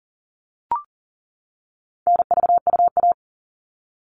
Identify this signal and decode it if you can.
Morse code, heard as a keyed tone, about 40 wpm, 700 Hz. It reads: NVUA
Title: All Cut Numbers - 4-Digit 40wpm